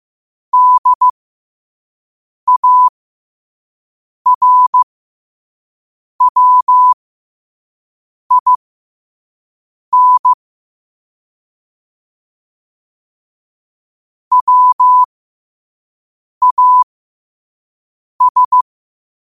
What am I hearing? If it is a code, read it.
Morse code, heard as a keyed tone, about 15 wpm, spaced as for 5 wpm, 1000 Hz. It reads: DARWIN WAS